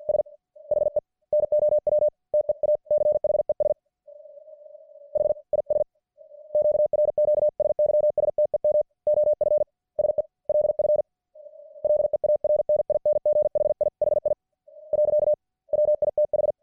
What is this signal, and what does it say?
Morse code, heard as a keyed tone, about 48 wpm, 600 Hz. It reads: S HE NOW TEACHES HIS QRQ SYSTEM OF SELF LEARNING HIHI . WITH